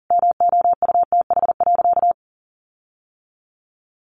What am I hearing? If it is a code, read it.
Morse code, heard as a keyed tone, about 40 wpm, 700 Hz. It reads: MOUTH.
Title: Sentences from Top 1500 Words 40wpm